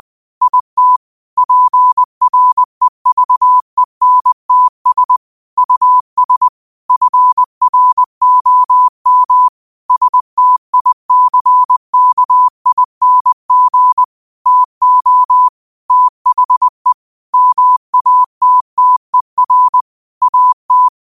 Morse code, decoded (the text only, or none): IT PREVENTS US FROM STICKING TO THE MATTER AT